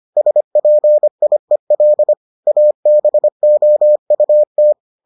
Morse code, heard as a keyed tone, about 25 wpm, 600 Hz. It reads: SPIEL ABOUT